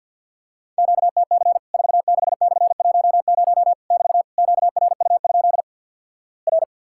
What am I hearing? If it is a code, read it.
Morse code, heard as a keyed tone, about 48 wpm, 700 Hz. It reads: QTC 46/10 <BT> QRU? R
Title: Worked All Europe DX Contest - QTC Exchange - 50wpm